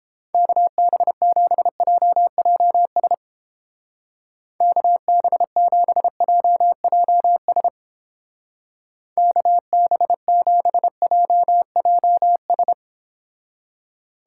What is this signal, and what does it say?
Morse code, heard as a keyed tone, about 31 wpm, 700 Hz. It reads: KB7JJS KB7JJS KB7JJS